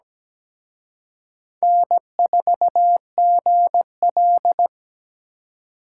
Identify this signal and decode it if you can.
Morse code, heard as a keyed tone, about 20 wpm, 700 Hz. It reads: N4GL
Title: Speed Racing - USA 1x2 Call Signs 17wpm